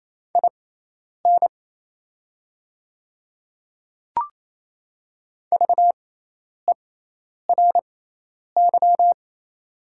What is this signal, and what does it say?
Morse code, heard as a keyed tone, about 28 wpm, spaced as for 10 wpm, 700 Hz. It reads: IN VERY
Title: 6x Character Spacing - Top 300 Words - 28wpm